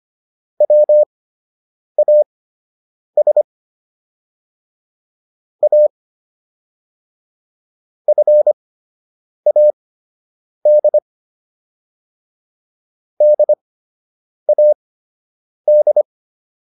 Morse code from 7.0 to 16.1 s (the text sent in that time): FAD DAD